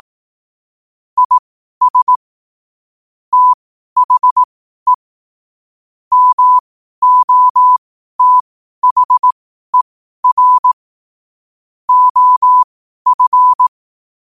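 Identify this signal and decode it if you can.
Morse code, heard as a keyed tone, about 18 wpm, spaced as for 12 wpm, 1000 Hz. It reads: IS THE MOTHER OF
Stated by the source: Morse code quotes at 12 words per minute.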